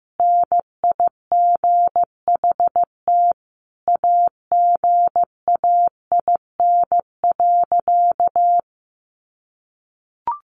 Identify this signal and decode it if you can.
Morse code, heard as a keyed tone, about 15 wpm, 700 Hz. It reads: NIGHT AGAIN.
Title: Sentences from Top 300 Words 15wpm